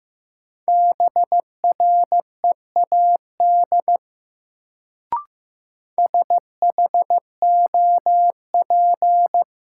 Morse code, heard as a keyed tone, about 15 wpm, 700 Hz. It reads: BREAD SHOP
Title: Sentences from Top 1600 Words - New Word Drill Alt - 15wpm